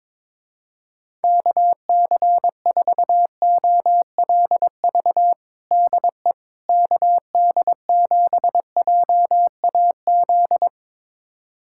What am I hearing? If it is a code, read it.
Morse code, heard as a keyed tone, about 22 wpm, 700 Hz. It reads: KC4OLV DE KD7JAZ
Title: USA Callsign DE Callsign - 6 Characters - 22wpm